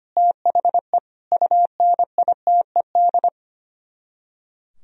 Morse code, heard as a keyed tone, about 28 wpm, 700 Hz. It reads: THE UNITED
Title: Speed-Racing - Sets of 4 Words 25wpm